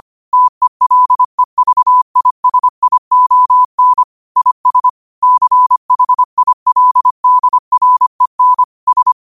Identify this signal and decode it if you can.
Morse code, heard as a keyed tone, about 25 wpm, 1000 Hz. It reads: TELEVISION IS CHILDREN S